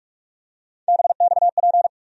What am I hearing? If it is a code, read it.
Morse code, heard as a keyed tone, about 43 wpm, 700 Hz. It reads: BXP
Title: USA Call Sign Postfixes - 3 Letters 45wpm